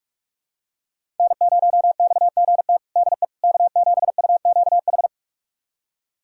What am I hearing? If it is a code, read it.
Morse code, heard as a keyed tone, about 44 wpm, 700 Hz. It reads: N0XGT DE K7UQH
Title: USA Callsign DE Callsign - 5 Characters - 45wpm